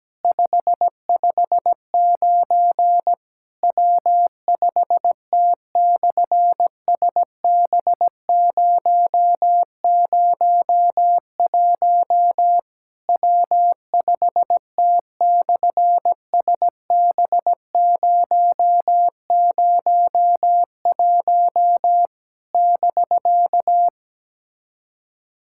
Morse code, heard as a keyed tone, about 17 wpm, 700 Hz. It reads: 559 W5T/SB001 W5T/SB001 <BK>
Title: SOTA QSO Practice 17wpm